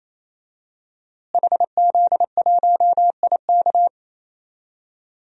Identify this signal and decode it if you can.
Morse code, heard as a keyed tone, about 27 wpm, 700 Hz. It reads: HZ1IK
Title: DXCC-100 Call Signs - 5 Characters - Repeated 3x - 28wpm